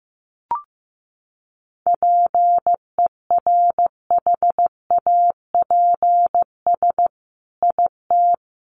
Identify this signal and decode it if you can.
Morse code, heard as a keyed tone, about 15 wpm, 700 Hz. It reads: PERHAPS IT
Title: Sentences from Top 1400 Words 15wpm